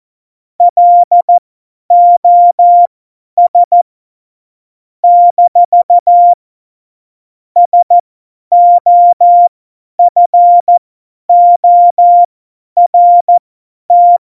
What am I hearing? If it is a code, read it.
Morse code, heard as a keyed tone, about 14 wpm, spaced as for 10 wpm, 700 Hz. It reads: LOS - SOFORT
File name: Morsekurs / Witze / Witz 18 August 2020, 50 F70